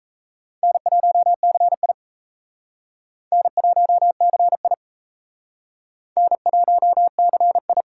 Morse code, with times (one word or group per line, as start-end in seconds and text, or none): 0.6-1.9 N1CI
3.3-4.7 N1CI
6.2-7.8 N1CI